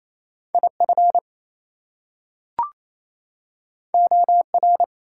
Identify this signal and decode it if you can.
Morse code, heard as a keyed tone, about 27 wpm, 700 Hz. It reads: IF OR
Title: Two Letter Words - 28wpm